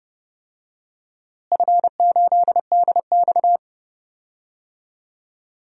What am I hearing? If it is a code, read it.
Morse code, heard as a keyed tone, about 32 wpm, 700 Hz. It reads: F8DX
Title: DXCC-100 Call Signs - 4 Characters - Repeated 2x - 30wpm